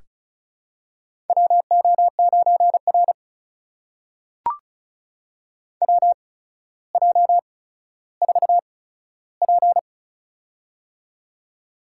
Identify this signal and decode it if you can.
Morse code, heard as a keyed tone, about 36 wpm, spaced as for 15 wpm, 700 Hz. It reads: WO9R WJ4P
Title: USA Call Signs - 4 Characters - 8x Character Spacing - 35wpm